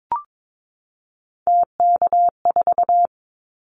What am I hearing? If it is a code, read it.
Morse code, heard as a keyed tone, about 22 wpm, 700 Hz. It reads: TK4
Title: Callsign Prefixes for Top 100 DXCC - 22wpm